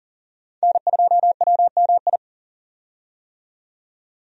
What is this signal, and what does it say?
Morse code, heard as a keyed tone, about 40 wpm, 700 Hz. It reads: N2WMI
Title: USA Call Signs - 5 Characters - 40wpm